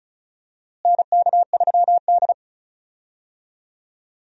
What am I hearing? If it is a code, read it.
Morse code, heard as a keyed tone, about 36 wpm, 700 Hz. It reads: NK3D
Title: USA Call Signs - 4 Characters - Repeated 2x - 35wpm